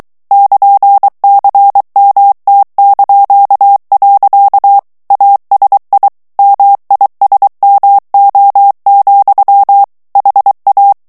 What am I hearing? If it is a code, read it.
Morse code, heard as a keyed tone, about 23 wpm, 800 Hz. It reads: <KN>CMT). ASI MISMO, HA
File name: morse mp3 / ESPAGNOL TEXTE EN CLAIR 2 VITESSE 22 MOTS MINUTES